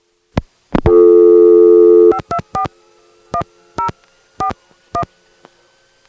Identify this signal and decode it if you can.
DTMF, heard as telephone touch tones, tones of about 90 ms. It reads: 5241041